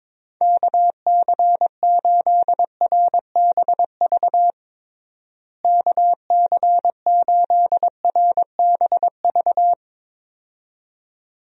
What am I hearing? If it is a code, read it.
Morse code, heard as a keyed tone, about 22 wpm, 700 Hz. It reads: KC8RBV KC8RBV
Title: USA Call Signs - 6 Characters - Repeated 2x - 22wpm